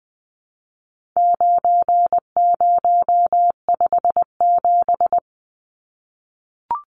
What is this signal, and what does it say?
Morse code, heard as a keyed tone, about 20 wpm, 700 Hz. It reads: 9057
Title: Numbers - 4-Digit 20wpm